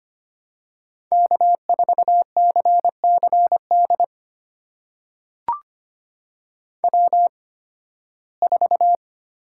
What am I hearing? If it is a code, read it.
Morse code, heard as a keyed tone, about 25 wpm, spaced as for 16 wpm, 700 Hz. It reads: K4CCD W4
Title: USA Call Signs - 5 Characters - 8x Character Spacing - 25wpm